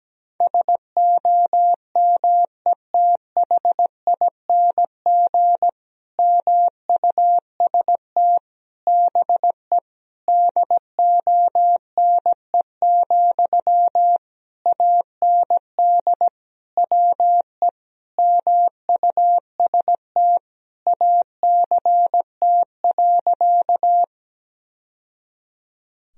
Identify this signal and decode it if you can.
Morse code, heard as a keyed tone, about 17 wpm, 700 Hz. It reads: SOMETHING MUST BE DONE, AND WE MUST ACT.
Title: Sentences from Top 1000 Words - Review Part 1 - 17wpm